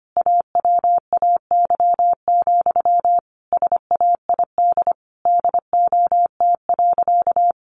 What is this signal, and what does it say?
Morse code, heard as a keyed tone, about 25 wpm, 700 Hz. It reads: AWAY, SAID DOT.
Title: Sentences from Top 1500 Words 25wpm